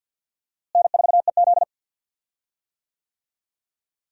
Morse code, heard as a keyed tone, about 51 wpm, 700 Hz. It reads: N4EZ
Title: USA Call Signs - 4 Characters - Repeated 3x - 50wpm